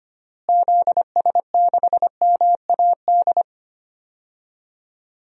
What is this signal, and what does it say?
Morse code, heard as a keyed tone, about 25 wpm, 700 Hz. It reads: ZS6MAD
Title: DXCC-100 Call Signs - 6 Characters - Repeated 2x - 25wpm